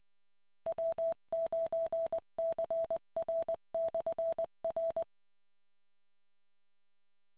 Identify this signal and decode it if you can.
Morse code, heard as a keyed tone, about 20 wpm, 660 Hz. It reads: W9CR/R